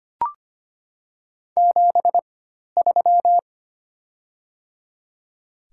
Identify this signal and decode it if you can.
Morse code, heard as a keyed tone, about 25 wpm, spaced as for 12 wpm, 700 Hz. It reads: 73
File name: Top 20 QSO Elements - 4x Character Spacing - 25wpm